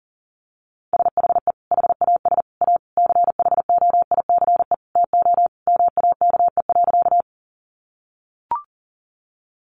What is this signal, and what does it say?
Morse code, heard as a keyed tone, about 37 wpm, 700 Hz. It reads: SHE HAS A CHOICE TO MAKE.
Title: Sentences from Top 900 Words 40wpm